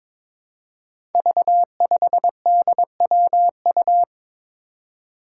Morse code, heard as a keyed tone, about 22 wpm, 700 Hz. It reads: V5DWU
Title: Five Challenging Letter-Number Combinations 22wpm